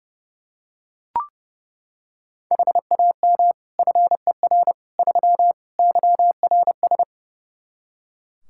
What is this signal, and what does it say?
Morse code, heard as a keyed tone, about 32 wpm, 700 Hz. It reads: HAM FER 3 YRS
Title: Practice QSO - Exchange 3 - More Info - 30wpm